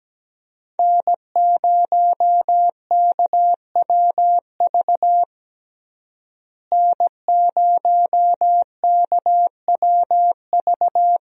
Morse code, heard as a keyed tone, about 17 wpm, 700 Hz. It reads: N0KWV N0KWV
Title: USA Call Signs - 5 Characters - Repeated 3x - 17wpm